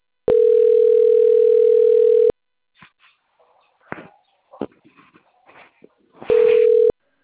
Ringback tone (North America), 440 + 480 Hz.